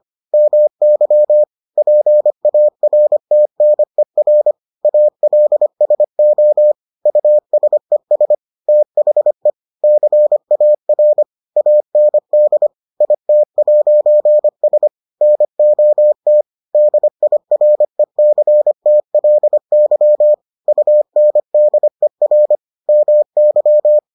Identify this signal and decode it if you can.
Morse code, heard as a keyed tone, about 25 wpm, 600 Hz. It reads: MY PARTNER ALSO USES THE CAR AND IT'S NOT DIRECTLY UNDER MY